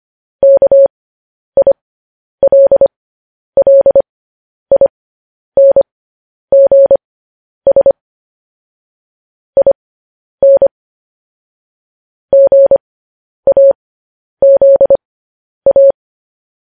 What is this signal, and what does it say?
Morse code, heard as a keyed tone, about 25 wpm, spaced as for 10 wpm, 550 Hz. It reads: KILLINGS IN GAZA
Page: news headlines in morse code at 10 wpm